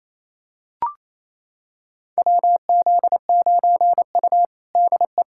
Morse code, heard as a keyed tone, about 28 wpm, 700 Hz. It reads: WZ9U DE
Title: USA Callsign DE Callsign - 4 Characters - 28wpm